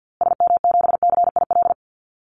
Morse code, heard as a keyed tone, about 50 wpm, 700 Hz. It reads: SM7CIL
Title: Call Signs Repeated 2x - 50wpm